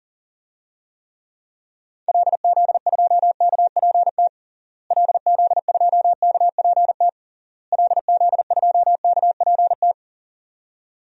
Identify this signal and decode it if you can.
Morse code, heard as a keyed tone, about 40 wpm, 700 Hz. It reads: LZ2KPT LZ2KPT LZ2KPT
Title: DXCC-100 Call Signs - 6 Characters - Repeated 3x - 40wpm